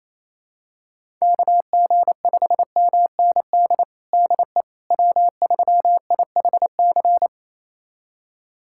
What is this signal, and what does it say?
Morse code, heard as a keyed tone, about 27 wpm, 700 Hz. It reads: KG5MND DE W3IHC
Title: USA Callsign DE Callsign 28wpm